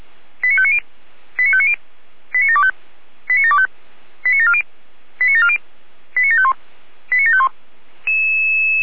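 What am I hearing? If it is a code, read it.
Five-tone selective call (ZVEI), heard as quick runs of five tones, tones of about 70 ms. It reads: 89590 89590 89725 89725 89740 89740 89751 89751
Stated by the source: Five-tone ZVEI selective call received on an SDR.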